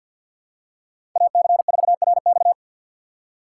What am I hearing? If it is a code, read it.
Morse code, heard as a keyed tone, about 46 wpm, 700 Hz. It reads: AC4RX
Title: USA Call Signs - 5 Characters - 3x Character Spacing - 50wpm